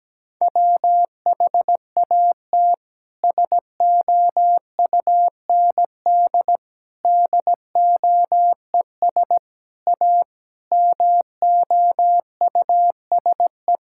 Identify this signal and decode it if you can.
Morse code, heard as a keyed tone, about 17 wpm, 700 Hz. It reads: WHAT SOUND DOES A MOUSE
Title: Sentences from Top 1000 Words - Review Part 5 - 17wpm